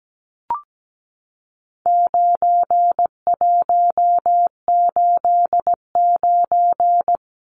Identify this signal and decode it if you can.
Morse code, heard as a keyed tone, about 17 wpm, 700 Hz. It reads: 9189
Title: Numbers - 4-Digit 17wpm